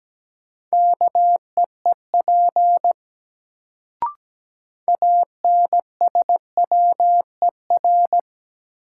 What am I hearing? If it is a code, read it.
Morse code, heard as a keyed tone, about 17 wpm, 700 Hz. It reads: KEEP ANSWER
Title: Sentences from Top 300 Words - New Word Drill - 17wpm